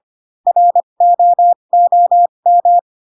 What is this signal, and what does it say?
Morse code, heard as a keyed tone, about 25 wpm, spaced as for 22 wpm, 700 Hz. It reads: ROOM
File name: Pinocchio / Pinocchio ch4 18 25 ik1hge